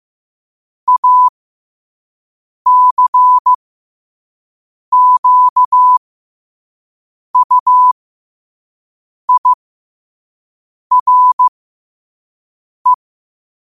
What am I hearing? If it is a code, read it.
Morse code, heard as a keyed tone, about 15 wpm, spaced as for 5 wpm, 1000 Hz. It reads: ACQUIRE
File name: qotdfiles / 5wpm